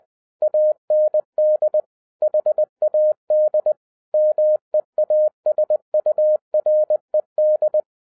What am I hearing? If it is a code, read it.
Morse code, heard as a keyed tone, about 20 wpm, 610 Hz. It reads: AND HAD MEASURED
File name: 2011-10-mp3 / n03-TARO JR0QWW 20wpm